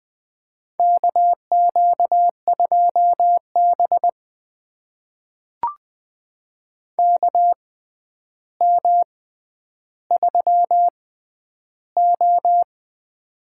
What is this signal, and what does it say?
Morse code, heard as a keyed tone, about 20 wpm, spaced as for 10 wpm, 700 Hz. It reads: KQ2B KM3O
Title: USA Call Signs - 4 Characters - 6x Character Spacing - 20wpm